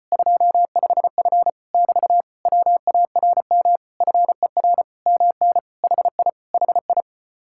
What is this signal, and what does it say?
Morse code, heard as a keyed tone, about 37 wpm, 700 Hz. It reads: 25F <BT> WARM FER MN HI HI